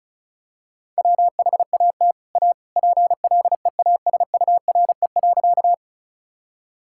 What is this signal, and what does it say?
Morse code, heard as a keyed tone, about 34 wpm, 700 Hz. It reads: WHAT A PLEASURE.
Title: Sentences from Top 1600 Words 35wpm